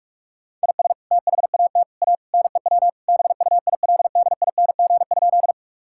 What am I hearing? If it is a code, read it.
Morse code, heard as a keyed tone, about 46 wpm, 700 Hz. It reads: IS THAT A NEW BUILDING?
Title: Sentences from Top 1000 Words - Review Part 5 - 45wpm